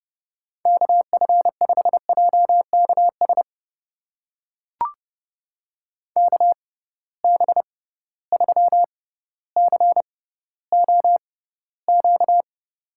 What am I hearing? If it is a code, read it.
Morse code, heard as a keyed tone, about 31 wpm, spaced as for 15 wpm, 700 Hz. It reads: KF5JKS KB3COQ
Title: USA Call Signs - 6 Characters - 6x Character Spacing - 30wpm